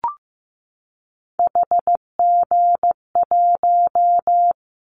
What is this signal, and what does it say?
Morse code, heard as a keyed tone, about 15 wpm, 700 Hz. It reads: HG1
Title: Callsign Prefixes for Top 50 DXCC - 15wpm